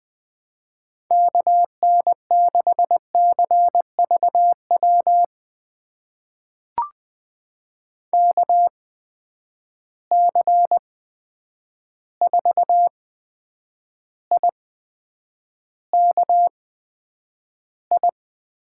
Morse code, heard as a keyed tone, about 21 wpm, spaced as for 9 wpm, 700 Hz. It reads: KN6CVW KC4IKI
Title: USA Call Signs - 6 Characters - 8x Character Spacing - 20wpm